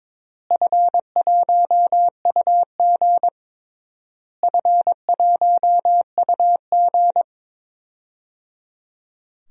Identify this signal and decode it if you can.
Morse code, heard as a keyed tone, about 22 wpm, 700 Hz. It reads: F1UG F1UG
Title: DXCC-100 Call Signs - 4 Characters - Repeated 3x - 22wpm